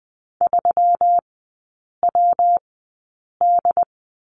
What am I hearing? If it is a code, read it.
Morse code, heard as a keyed tone, about 20 wpm, spaced as for 8 wpm, 700 Hz. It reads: 3WD
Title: 127 Lesson 21e - Number 6 - Three Characters 20wpm